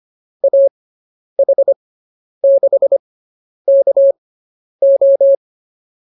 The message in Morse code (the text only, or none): AH6KO